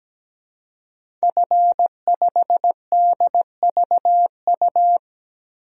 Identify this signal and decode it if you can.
Morse code, heard as a keyed tone, about 17 wpm, 700 Hz. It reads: F5DVU
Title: DXCC-100 Call Signs - 5 Characters - 17wpm